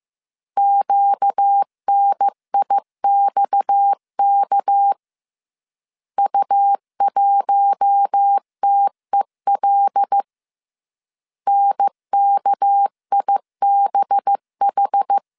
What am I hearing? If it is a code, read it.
Morse code, heard as a keyed tone, about 15 wpm, 800 Hz. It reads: QNIXK U1TEL NKIBH